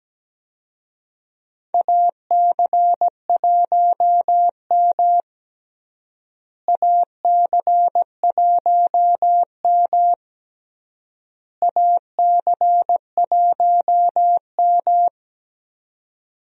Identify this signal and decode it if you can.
Morse code, heard as a keyed tone, about 17 wpm, 700 Hz. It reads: AC1M AC1M AC1M
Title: USA Call Signs - 4 Characters - Repeated 3x - 17wpm